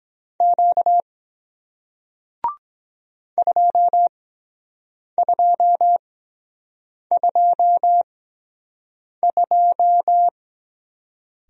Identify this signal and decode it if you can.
Morse code, heard as a keyed tone, about 23 wpm, 700 Hz. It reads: Q 2 2 2 2